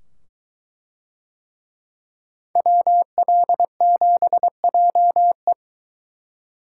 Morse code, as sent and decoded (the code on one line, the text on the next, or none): .-- .-.. --... .--- .
WL7JE